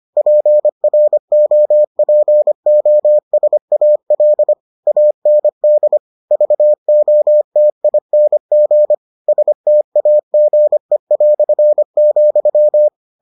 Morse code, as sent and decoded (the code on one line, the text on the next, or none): .--. .-. --- .--. --- ... .- .-.. / .- -. -.. / ...- --- - .. -. --. / ... - .- --. .-..-. --..--
PROPOSAL AND VOTING STAGE",